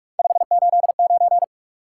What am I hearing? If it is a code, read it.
Morse code, heard as a keyed tone, about 44 wpm, 700 Hz. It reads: 589
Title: Top 180 QSO Elements 45wpm